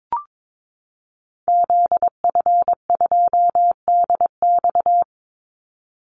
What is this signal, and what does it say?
Morse code, heard as a keyed tone, about 22 wpm, 700 Hz. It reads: ZF2DX
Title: Call Signs - 22wpm